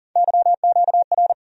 Morse code, heard as a keyed tone, about 40 wpm, 700 Hz. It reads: YQR
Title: USA Call Sign Postfixes - 3 Letters 40wpm